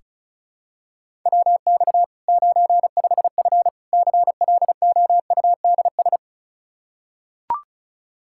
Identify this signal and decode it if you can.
Morse code, heard as a keyed tone, about 36 wpm, 700 Hz. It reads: WX 95F CLOUDS
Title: K1USN SST QSO Practice 35wpm